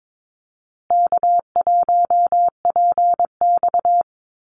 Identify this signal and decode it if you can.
Morse code, heard as a keyed tone, about 22 wpm, 700 Hz. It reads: K1PX
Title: Easy Call Signs Repeated 3x - 22wpm